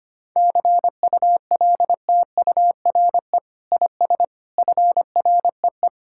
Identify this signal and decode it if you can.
Morse code, heard as a keyed tone, about 25 wpm, 700 Hz. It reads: CULTURE IS FREE
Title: Sentences from Top 1000 Words - Review Part 6 - 25wpm